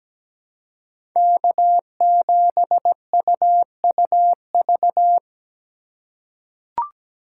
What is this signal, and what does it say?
Morse code, heard as a keyed tone, about 17 wpm, 700 Hz. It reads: K7UUV
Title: USA Call Signs - 5 Characters - 2x Character Spacing - 17wpm